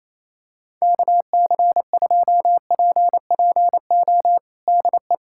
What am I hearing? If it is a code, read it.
Morse code, heard as a keyed tone, about 28 wpm, 700 Hz. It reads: KC2PPO DE
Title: USA Callsign DE Callsign - 6 Characters - 28wpm